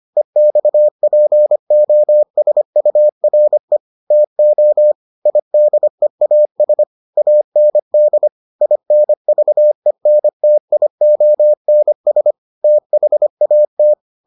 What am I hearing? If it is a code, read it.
Morse code, heard as a keyed tone, about 25 wpm, 600 Hz. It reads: EXPOSURE TO IDEAS AND INVENTIONS THAT